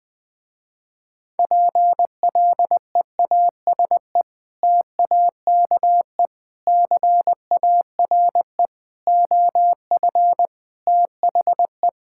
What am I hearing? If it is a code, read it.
Morse code, heard as a keyed tone, about 20 wpm, 700 Hz. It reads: PLEASE TAKE CARE OF THE
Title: Sentences with 3-Word Phrases - Spoken Beginning - 20wpm